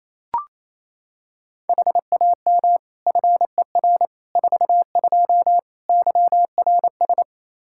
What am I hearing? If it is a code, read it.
Morse code, heard as a keyed tone, about 28 wpm, 700 Hz. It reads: HAM FER 42 YRS
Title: Practice QSO - Exchange 3 - More Info - 28wpm